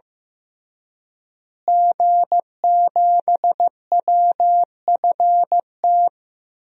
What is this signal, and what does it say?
Morse code, heard as a keyed tone, about 15 wpm, 700 Hz. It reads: G7WFT
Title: DXCC-100 Call Signs - 5 Characters - Repeated 2x - 15wpm